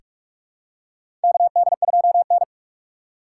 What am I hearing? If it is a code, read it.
Morse code, heard as a keyed tone, about 44 wpm, 700 Hz. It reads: KD2N
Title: USA Call Signs - 4 Characters - 6x Character Spacing - 45wpm